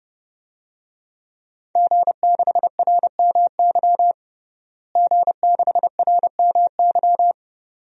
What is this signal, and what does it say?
Morse code, heard as a keyed tone, about 30 wpm, 700 Hz. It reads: G6RMY G6RMY
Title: DXCC-100 Call Signs - 5 Characters - Repeated 2x - 30wpm